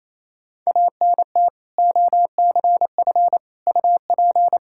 Morse code, heard as a keyed tone, about 28 wpm, 700 Hz. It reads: ANT OCF UP
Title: K1USN SST QSO Practice 28wpm